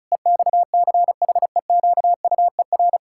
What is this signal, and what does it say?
Morse code, heard as a keyed tone, about 36 wpm, 700 Hz. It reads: EXCHEQUER
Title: Pangram Challenge v2 - 35wpm